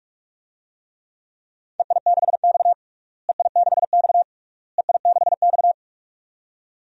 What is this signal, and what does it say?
Morse code, heard as a keyed tone, about 46 wpm, 700 Hz. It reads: EI6X EI6X EI6X
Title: DXCC-100 Call Signs - 4 Characters - Repeated 3x - 45wpm